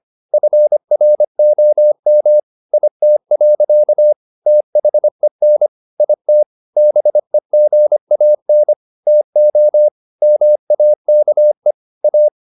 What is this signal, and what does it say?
Morse code, heard as a keyed tone, about 25 wpm, 600 Hz. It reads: FROM IT. THEN IT BEGAN TO MAKE A
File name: the old man and the sea part 1 25wpm